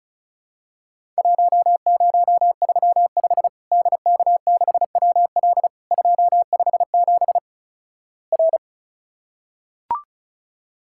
Morse code, heard as a keyed tone, about 36 wpm, 700 Hz. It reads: 1035 DK6WL 257 R
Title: Worked All Europe DX Contest - QTC Exchange - 35wpm